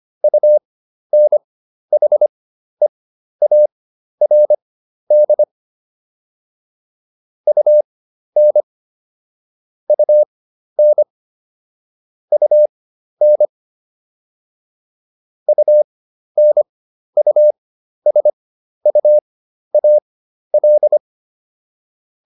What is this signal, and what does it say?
Morse code, heard as a keyed tone, about 25 wpm, spaced as for 12 wpm, 600 Hz. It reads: UNHEARD UN UN UN UNUSUAL